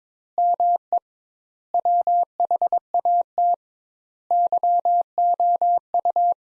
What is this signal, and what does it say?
Morse code, heard as a keyed tone, about 22 wpm, 700 Hz. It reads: ME WHAT YOU
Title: Sentences Top 200 Words - 22wpm